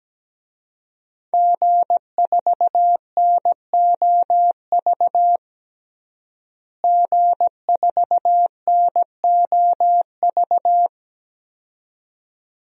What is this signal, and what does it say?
Morse code, heard as a keyed tone, about 17 wpm, 700 Hz. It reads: G4NOV G4NOV
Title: DXCC-100 Call Signs - 5 Characters - Repeated 2x - 17wpm